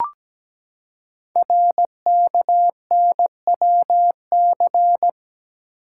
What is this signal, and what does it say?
Morse code, heard as a keyed tone, about 17 wpm, 700 Hz. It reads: RKNWC
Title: Code Groups - 5 Letters - 17wpm